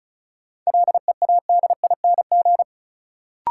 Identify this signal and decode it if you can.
Morse code, heard as a keyed tone, about 36 wpm, 700 Hz. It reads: LEADING
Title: Sentences from Top 1100 Words - New Word Drill - 35wpm